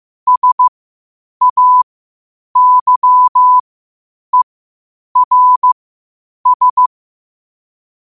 Morse code, heard as a keyed tone, about 15 wpm, spaced as for 8 wpm, 1000 Hz. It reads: SAYERS
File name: qotdfiles / 8wpm